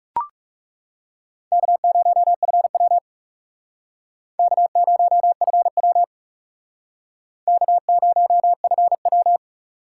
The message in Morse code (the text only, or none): K0FW K0FW K0FW